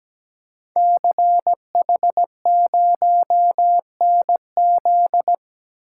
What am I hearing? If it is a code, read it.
Morse code, heard as a keyed tone, about 17 wpm, 700 Hz. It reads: CH0NZ
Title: Code Groups - 5 Letter:Numbers - 17wpm